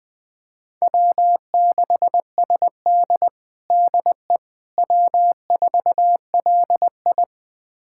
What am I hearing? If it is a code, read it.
Morse code, heard as a keyed tone, about 20 wpm, 700 Hz. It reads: W6SD DE W4LI
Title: USA Callsign DE Callsign - 4 Characters - 20wpm